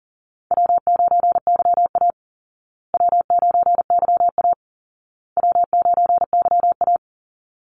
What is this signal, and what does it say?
Morse code, heard as a keyed tone, about 40 wpm, 700 Hz. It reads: W9YA W9YA W9YA
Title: Easy Call Signs Repeated 3x - 40wpm